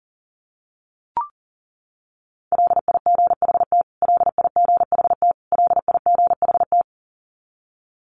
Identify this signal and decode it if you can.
Morse code, heard as a keyed tone, about 40 wpm, 700 Hz. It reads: LIGHT LIGHT LIGHT
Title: Top 200 Words Repeated 40wpm